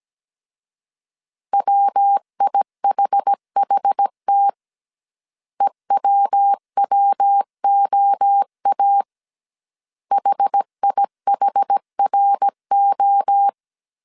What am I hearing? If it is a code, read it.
Morse code, heard as a keyed tone, about 17 wpm, 800 Hz. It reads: WIHHT EWWOA HIHRO